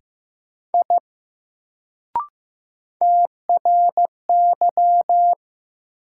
Morse code, heard as a keyed tone, about 15 wpm, 700 Hz. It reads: I TRY
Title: Sentences from Top 200 Words - New Word Drill - 15wpm